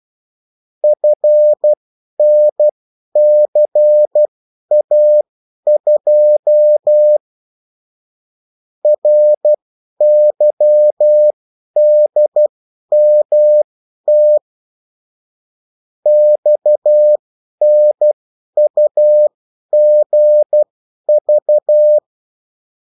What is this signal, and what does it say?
Morse code, heard as a keyed tone, about 12 wpm, spaced as for 10 wpm, 600 Hz. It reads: FNCA2 RYDMT XNUGV
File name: morse mp3 / revision9-vitesse 10 mots